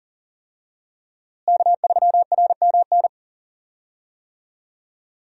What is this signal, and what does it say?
Morse code, heard as a keyed tone, about 40 wpm, 700 Hz. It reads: K3RMN
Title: USA Call Signs - 5 Characters - Repeated 3x - 40wpm